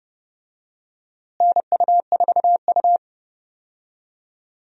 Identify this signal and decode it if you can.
Morse code, heard as a keyed tone, about 31 wpm, 700 Hz. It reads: NU4U
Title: USA Call Signs - 4 Characters - Repeated 3x - 30wpm